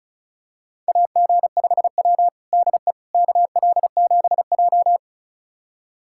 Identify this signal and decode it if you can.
Morse code, heard as a keyed tone, about 34 wpm, 700 Hz. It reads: AG5W DE KL7J
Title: USA Callsign DE Callsign - 4 Characters - 35wpm